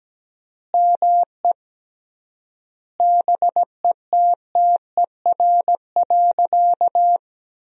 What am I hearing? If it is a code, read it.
Morse code, heard as a keyed tone, about 17 wpm, 700 Hz. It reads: ME BETTER.
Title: 3x Word-Spacing - Sentences from Top 400 Words - 17wpm